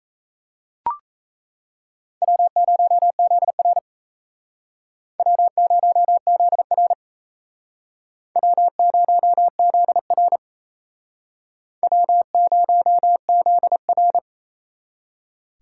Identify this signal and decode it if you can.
Morse code, heard as a keyed tone, about 35 wpm, 700 Hz. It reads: W0ZR W0ZR W0ZR W0ZR
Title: Speed Racing - USA 1x2 Call Signs 28wpm